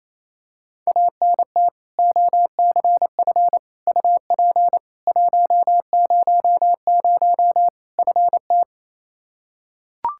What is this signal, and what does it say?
Morse code, heard as a keyed tone, about 28 wpm, 700 Hz. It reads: ANT OCF UP 100 FT
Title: Practice QSO - Exchange 2 - Info - 28wpm